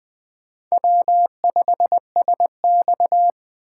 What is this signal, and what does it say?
Morse code, heard as a keyed tone, about 18 wpm, 700 Hz. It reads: W5SX
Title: Speed Racing - USA 1x2 Call Signs 15wpm